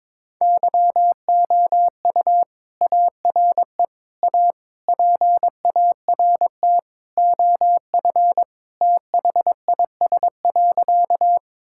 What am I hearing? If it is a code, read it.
Morse code, heard as a keyed tone, about 22 wpm, 700 Hz. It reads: YOU ARE A PART OF THIS.
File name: practice / 22wpm / Sentences with 3-Word Phrases - 22wpm